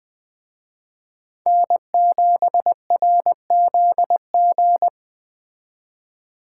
Morse code, heard as a keyed tone, about 21 wpm, 700 Hz. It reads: N7RZG
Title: Speed Racing - USA 1x3 Call Signs 17wpm